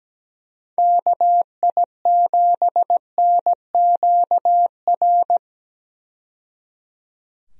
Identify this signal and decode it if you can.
Morse code, heard as a keyed tone, about 17 wpm, 700 Hz. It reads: KI7NQR
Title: USA Call Signs - 6 Characters - 17wpm